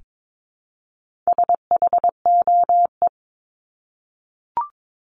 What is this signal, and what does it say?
Morse code, heard as a keyed tone, about 22 wpm, 700 Hz. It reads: SHOE
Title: Top 1751-2000 Words 22wpm